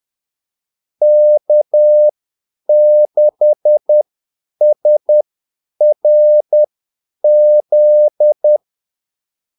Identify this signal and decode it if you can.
Morse code, heard as a keyed tone, about 10 wpm, spaced as for 8 wpm, 600 Hz. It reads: K6SRZ
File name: Callsigns-10-8wpm